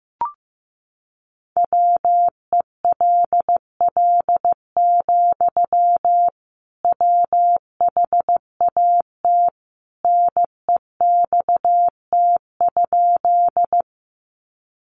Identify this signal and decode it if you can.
Morse code, heard as a keyed tone, about 15 wpm, 700 Hz. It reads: WELL, WHAT NEXT?
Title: Sentences from Top 300 Words 15wpm